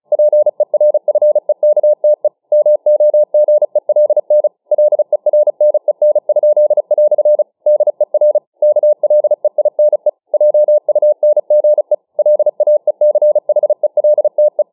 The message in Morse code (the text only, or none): PERFEKTE MOGELN LERNEN?" DER KLEINE JUNGE LAECHELTE